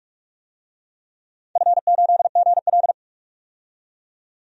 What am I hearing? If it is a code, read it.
Morse code, heard as a keyed tone, about 46 wpm, 700 Hz. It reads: F8GL
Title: DXCC-100 Call Signs - 4 Characters - 45wpm